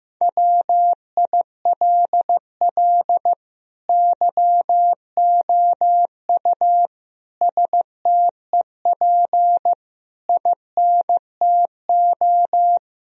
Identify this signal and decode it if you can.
Morse code, heard as a keyed tone, about 15 wpm, 700 Hz. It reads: WILL YOU STEP INTO
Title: Sentences from Top 1000 Words - Review Part 2 - 15wpm